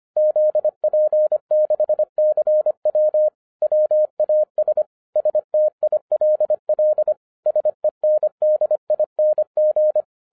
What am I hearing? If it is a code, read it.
Morse code, heard as a keyed tone, about 25 wpm, 610 Hz. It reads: ZP6CW WAS STILL SENDING
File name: FIRST FISTS CONTACT-25wpm